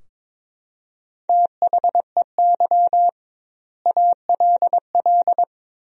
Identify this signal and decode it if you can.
Morse code, heard as a keyed tone, about 22 wpm, 700 Hz. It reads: THEY ALL
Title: 2x Word-Spacing - Sentences from Top 500 Words - 22wpm